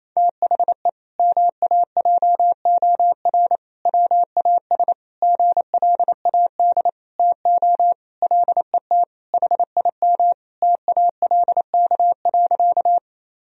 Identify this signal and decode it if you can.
Morse code, heard as a keyed tone, about 28 wpm, 700 Hz. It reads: THE MAJOR WAS GLAD TO LET HIM TALK.
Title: Sentences from Top 1000 Words - Review Part 4 - 28wpm